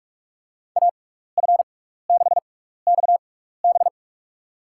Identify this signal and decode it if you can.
Morse code, heard as a keyed tone, about 46 wpm, spaced as for 15 wpm, 700 Hz. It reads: AF6XB
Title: USA Call Signs - 5 Characters - 6x Character Spacing - 45wpm